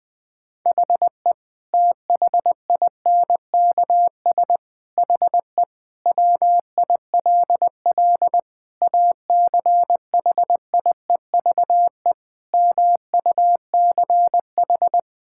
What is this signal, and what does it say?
Morse code, heard as a keyed tone, about 20 wpm, 700 Hz. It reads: HE THINKS HE WILL ACHIEVE MUCH
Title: Sentences from Top 1000 Words - Review Part 6 - 20wpm